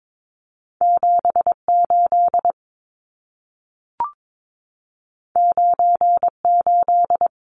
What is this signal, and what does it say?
Morse code, heard as a keyed tone, about 22 wpm, 700 Hz. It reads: 78 98
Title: Numbers - 2-Digit 22wpm